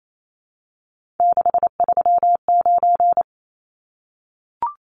Morse code, heard as a keyed tone, about 28 wpm, 700 Hz. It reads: 639
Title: Numbers - 3-Digit 28wpm